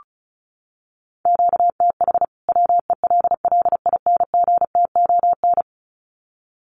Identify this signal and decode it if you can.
Morse code, heard as a keyed tone, about 34 wpm, 700 Hz. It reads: QTH WELLINGTON
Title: Common QSO Phrases 35wpm